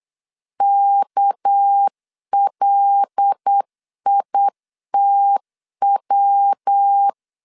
Morse code, heard as a keyed tone, about 8 wpm, 800 Hz. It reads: KLITW
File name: LEZIONI / 8 YJ45 / 8 YJ4 8wpm 3246